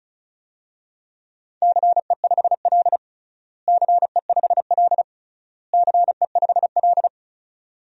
Morse code, heard as a keyed tone, about 36 wpm, 690 Hz. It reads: CE5L CE5L CE5L
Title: DXCC-100 Call Signs - 4 Characters - Repeated 3x - 35wpm